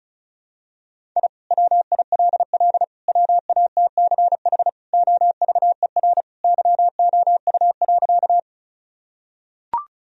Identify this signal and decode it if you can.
Morse code, heard as a keyed tone, about 36 wpm, 700 Hz. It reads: I WILL WATCH OVER YOU.
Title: Sentences from Top 1000 Words - Review Part 1 - 35wpm